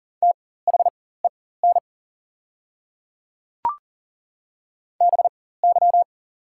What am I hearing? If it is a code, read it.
Morse code, heard as a keyed tone, about 40 wpm, spaced as for 19 wpm, 690 Hz. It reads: THEN BY
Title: 4x Character Spacing - Top 200 Words - 40wpm